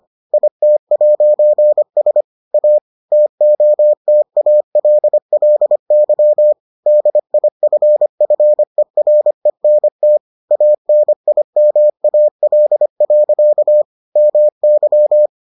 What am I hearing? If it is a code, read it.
Morse code, heard as a keyed tone, about 25 wpm, 600 Hz. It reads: IT'S A TOTALLY DIFFERENT ANIMAL. MY